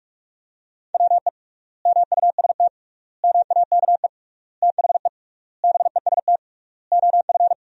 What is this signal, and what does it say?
Morse code, heard as a keyed tone, about 46 wpm, 700 Hz. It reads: WE MUST MAKE THE BEST OF
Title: 3x Word-Spacing - Sentences from Top 400 Words - 45wpm